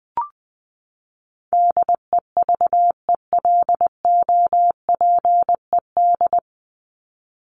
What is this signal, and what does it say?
Morse code, heard as a keyed tone, about 20 wpm, 700 Hz. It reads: DEVELOPED
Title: ED Suffix - Top 100 Words with Base Words 20wpm